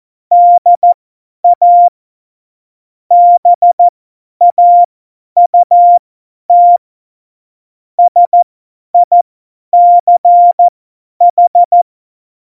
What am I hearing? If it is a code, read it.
Morse code, heard as a keyed tone, about 14 wpm, spaced as for 10 wpm, 700 Hz. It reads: DA BAUT SICH
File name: Morsekurs / Witze / Witz 03. November 2020, 50 F70